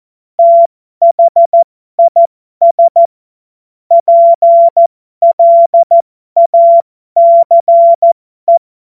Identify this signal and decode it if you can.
Morse code, heard as a keyed tone, about 14 wpm, spaced as for 12 wpm, 680 Hz. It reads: THIS PLACE
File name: Morsekurs / Witze